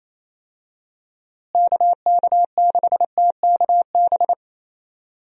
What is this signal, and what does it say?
Morse code, heard as a keyed tone, about 28 wpm, 700 Hz. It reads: KK6TKB
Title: USA Call Signs - 6 Characters - Repeated 3x - 28wpm